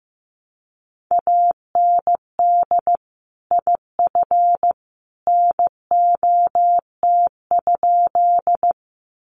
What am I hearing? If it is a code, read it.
Morse code, heard as a keyed tone, about 15 wpm, 700 Hz. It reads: AND IF NOT?
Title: Sentences from Top 600 Words 15wpm